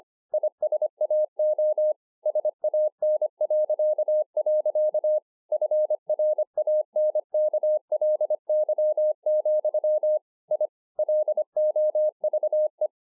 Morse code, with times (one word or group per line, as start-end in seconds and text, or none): 0.3-1.9 ISAO
2.3-5.2 SAN..
5.5-10.2 FRANKLY,
10.5-10.7 I
11.0-12.9 LOVE